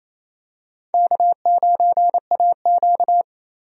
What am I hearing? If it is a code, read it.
Morse code, heard as a keyed tone, about 28 wpm, 700 Hz. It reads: K9AQ
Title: USA Call Signs - 4 Characters - Repeated 2x - 28wpm